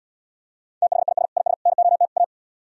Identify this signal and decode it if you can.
Morse code, heard as a keyed tone, about 77 wpm, 700 Hz. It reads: THIS IS NOT A